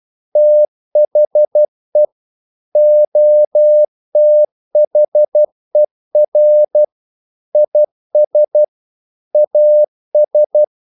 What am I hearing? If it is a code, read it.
Morse code, heard as a keyed tone, about 12 wpm, 600 Hz. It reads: THE OTHER IS AS
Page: English quotations in CW (telegraphy - morse)